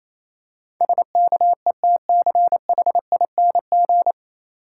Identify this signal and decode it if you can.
Morse code, heard as a keyed tone, about 27 wpm, 690 Hz. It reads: SKETCHING
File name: Easy Words - Speed-Racing with Hints - 22wpm